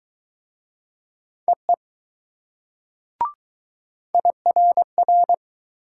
Morse code, heard as a keyed tone, about 19 wpm, 700 Hz. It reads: EE IRR